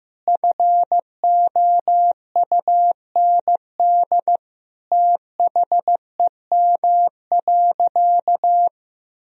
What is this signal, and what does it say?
Morse code, heard as a keyed tone, about 15 wpm, 700 Hz. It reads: FOUND THEM.
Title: Sentences from Top 1000 Words - Review Part 1 - 15wpm